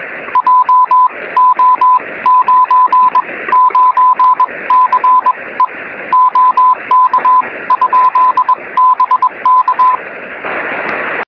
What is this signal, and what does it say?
Morse code, heard as a keyed tone, about 21 wpm, spaced as for 18 wpm, 1000 Hz. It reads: JO99CE OK? BK
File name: sm7lcb / 09-sm0nue 23cm